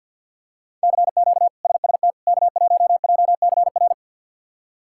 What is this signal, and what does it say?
Morse code, heard as a keyed tone, about 50 wpm, 700 Hz. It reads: CQ SST K1JCR
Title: K1USN SST QSO Practice - 50wpm